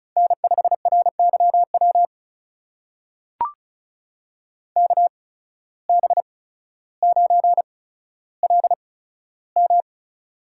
Morse code, heard as a keyed tone, about 36 wpm, spaced as for 15 wpm, 700 Hz. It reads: N5RYW KB9LM
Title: USA Call Signs - 5 Characters - 8x Character Spacing - 35wpm